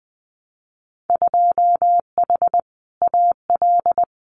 Morse code, heard as a keyed tone, about 19 wpm, 700 Hz. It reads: 2H AL
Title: Winter Field Day Exchanges 20wpm